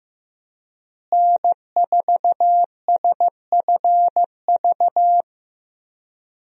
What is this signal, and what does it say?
Morse code, heard as a keyed tone, about 15 wpm, 700 Hz. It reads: N4SFV
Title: USA Call Signs - 5 Characters - Repeated 2x - 15wpm